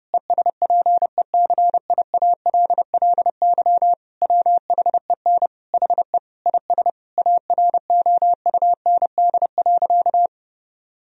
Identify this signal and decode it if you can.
Morse code, heard as a keyed tone, about 30 wpm, 700 Hz. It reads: ESPECIALLY WHEN HE IS AROUND.
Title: Sentences from Top 1000 Words - Review Part 4 - 30wpm